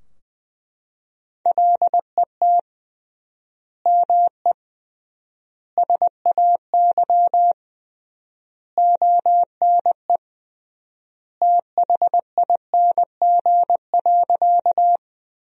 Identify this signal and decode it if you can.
Morse code, heard as a keyed tone, about 20 wpm, 700 Hz. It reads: LET ME SAY ONE THING.
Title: 3x Word-Spacing - Sentences from Top 300 Words - 20wpm